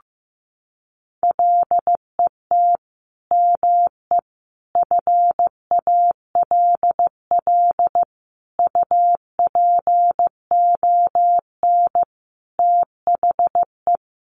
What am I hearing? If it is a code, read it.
Morse code, heard as a keyed tone, about 15 wpm, 700 Hz. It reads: LET ME FALL UPON THE
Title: Sentences from Top 1100 Words 15wpm